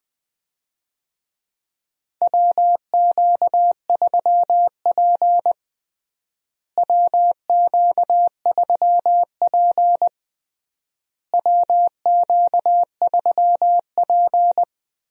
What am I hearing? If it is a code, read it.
Morse code, heard as a keyed tone, about 20 wpm, 700 Hz. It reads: WQ3P WQ3P WQ3P
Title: USA Call Signs - 4 Characters - Repeated 3x - 20wpm